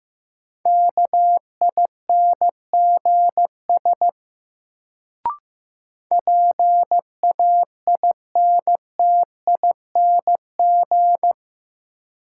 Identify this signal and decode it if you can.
Morse code, heard as a keyed tone, about 15 wpm, 700 Hz. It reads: KINGS PAINTING
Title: Sentences from Top 1000 Words - Review Part 6 - New Word Drill - 15wpm